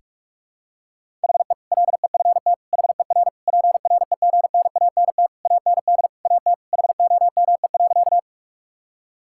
Morse code, heard as a keyed tone, about 46 wpm, 700 Hz. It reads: HE LEFT HER PREGNANT AND AT HOME.
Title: Sentences from Top 1000 Words - Review Part 6 - 45wpm